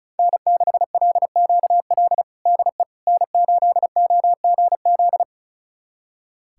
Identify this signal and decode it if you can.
Morse code, heard as a keyed tone, about 36 wpm, 700 Hz. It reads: N6LQL DE N8OGZ
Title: USA Callsign DE Callsign - 5 Characters - 35wpm